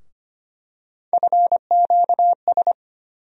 Morse code, heard as a keyed tone, about 25 wpm, 700 Hz. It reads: FQS
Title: USA Call Sign Postfixes - 3 Letters 25wpm